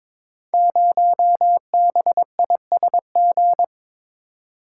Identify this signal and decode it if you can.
Morse code, heard as a keyed tone, about 22 wpm, 700 Hz. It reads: 0BISG
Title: Code Groups - 5 Letter:Numbers - 22wpm